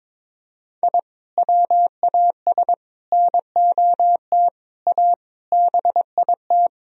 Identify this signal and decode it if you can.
Morse code, heard as a keyed tone, about 22 wpm, 700 Hz. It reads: I WAS NOT A BIT
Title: Sentences from Top 1000 Words - Review Part 4 - 22wpm